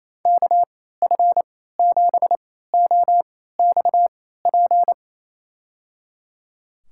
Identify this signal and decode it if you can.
Morse code, heard as a keyed tone, about 27 wpm, spaced as for 16 wpm, 700 Hz. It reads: KF7OXP
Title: USA Call Signs - 6 Characters - 3x Character Spacing - 28wpm